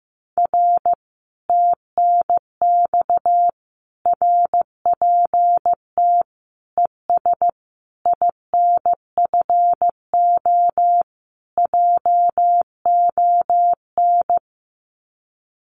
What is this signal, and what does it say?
Morse code, heard as a keyed tone, about 15 wpm, 700 Hz. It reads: R TNX RPT ES INFO JON
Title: Common QSO Phrases Encore 15wpm